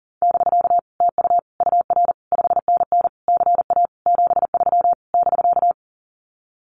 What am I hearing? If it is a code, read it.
Morse code, heard as a keyed tone, about 43 wpm, 700 Hz. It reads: <BK> TU UR 5NN CA 73 <BK>
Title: Common QSO Phrases Encore 40wpm